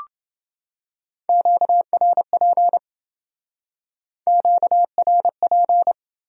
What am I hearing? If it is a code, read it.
Morse code, heard as a keyed tone, about 29 wpm, 700 Hz. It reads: QRP QRP